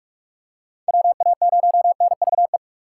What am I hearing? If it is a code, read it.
Morse code, heard as a keyed tone, about 47 wpm, 700 Hz. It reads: WA0NVE
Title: USA Call Signs - 3x Character Spacing - 45wpm